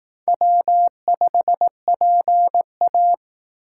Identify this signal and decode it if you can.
Morse code, heard as a keyed tone, about 18 wpm, 700 Hz. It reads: W5PA
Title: Speed Racing - USA 1x2 Call Signs 15wpm